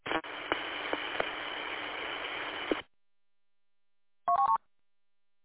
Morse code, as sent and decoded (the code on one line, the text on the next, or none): none